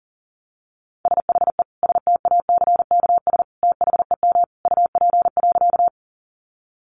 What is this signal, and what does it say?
Morse code, heard as a keyed tone, about 40 wpm, 700 Hz. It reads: SHE STACKS THEM UP.
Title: Easy Phrases 40wpm